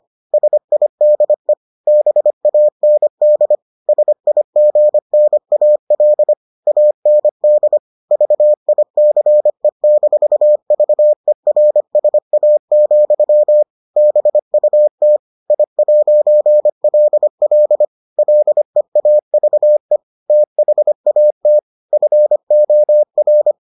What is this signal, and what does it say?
Morse code, heard as a keyed tone, about 25 wpm, 600 Hz. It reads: SIDE BAND SIGNAL AND VICE-VERSA, BUT I'LL LEAVE THAT FOR